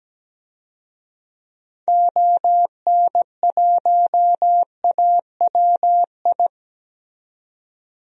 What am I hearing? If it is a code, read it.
Morse code, heard as a keyed tone, about 17 wpm, 700 Hz. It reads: ON1AWI
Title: DXCC-100 Call Signs - 6 Characters - 17wpm